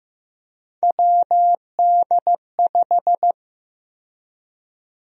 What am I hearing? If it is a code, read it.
Morse code, heard as a keyed tone, about 15 wpm, 700 Hz. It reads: WD5
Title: USA Call Sign Prefixes - Repeated 2x - 15wpm